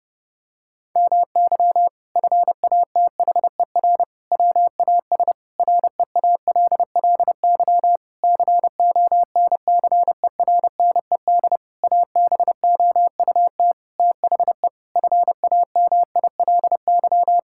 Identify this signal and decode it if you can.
Morse code, heard as a keyed tone, about 32 wpm, 700 Hz. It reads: MY FATHER WAS REALLY CONCERNED ABOUT THE FAMILY
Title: Sentences from Top 1000 Words - Review Part 6 - 30wpm